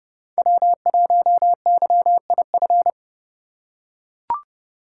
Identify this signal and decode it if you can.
Morse code, heard as a keyed tone, about 32 wpm, 700 Hz. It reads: W1YIF
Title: USA Call Signs - 5 Characters - 4x Character Spacing - 30wpm